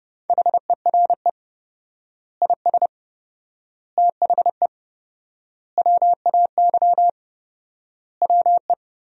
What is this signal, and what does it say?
Morse code, heard as a keyed tone, about 30 wpm, 700 Hz. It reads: HERE IS THE WAY WE
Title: 4x Word-Spacing - Sentences from Top 200 Words - 30wpm